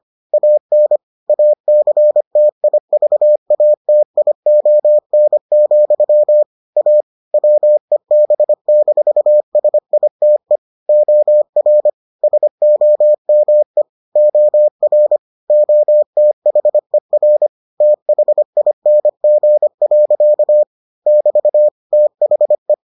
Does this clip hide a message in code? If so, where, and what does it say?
Yes: Morse code from 0.3 to 22.7 s, heard as a keyed tone, about 25 wpm, 600 Hz. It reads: AN ACTIVATION, A WEB-SITE OR SOME OR OTHER THING. <BT> THE